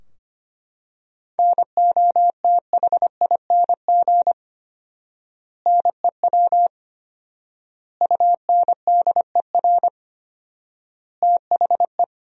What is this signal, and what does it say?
Morse code, heard as a keyed tone, about 25 wpm, 700 Hz. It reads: NOTHING NEW UNDER THE
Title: 4x Word-Spacing - Sentences from Top 500 Words - 25wpm